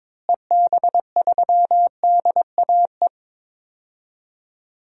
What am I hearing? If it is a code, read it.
Morse code, heard as a keyed tone, about 22 wpm, 700 Hz. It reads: EB3DAE
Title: DXCC-100 Call Signs - 6 Characters - Repeated 3x - 22wpm